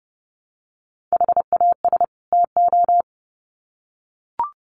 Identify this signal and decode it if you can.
Morse code, heard as a keyed tone, about 30 wpm, 700 Hz. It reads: HAS TO
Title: Sets of 2 Words 30wpm